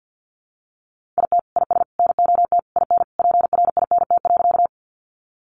Morse code, heard as a keyed tone, about 50 wpm, 700 Hz. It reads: IT IS NOT IN PRINT.
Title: Sentences from Top 1400 Words 50wpm